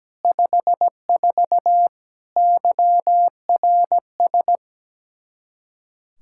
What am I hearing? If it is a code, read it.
Morse code, heard as a keyed tone, about 17 wpm, 700 Hz. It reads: 54 YRS
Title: K1USN SST QSO Practice 17wpm